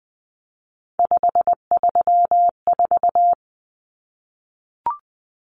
Morse code, heard as a keyed tone, about 20 wpm, 700 Hz. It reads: 534
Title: Common Cut Numbers - 3-Digit 20wpm